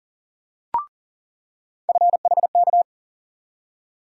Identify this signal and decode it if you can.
Morse code, heard as a keyed tone, about 40 wpm, 700 Hz. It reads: FHK
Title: USA Call Sign Postfixes - 3 Letters 40wpm